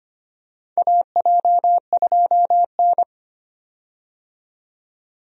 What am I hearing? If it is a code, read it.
Morse code, heard as a keyed tone, about 25 wpm, 700 Hz. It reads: AJ2N
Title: USA Call Signs - 4 Characters - 25wpm